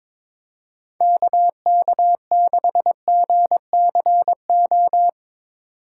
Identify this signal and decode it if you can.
Morse code, heard as a keyed tone, about 22 wpm, 700 Hz. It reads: KK6GCO
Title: USA Call Signs - 6 Characters - 4x Character Spacing - 22wpm